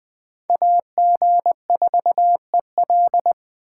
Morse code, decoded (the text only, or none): AG4EL